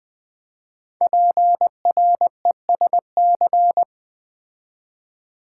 Morse code, heard as a keyed tone, about 20 wpm, 700 Hz. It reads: PRESC
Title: Build A Word - 5 Letters At A Time - 12-Letter Words - 20wpm